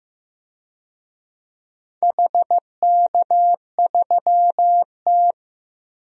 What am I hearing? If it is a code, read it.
Morse code, heard as a keyed tone, about 15 wpm, 700 Hz. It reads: HK3T
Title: DXCC-100 Call Signs - 4 Characters - 15wpm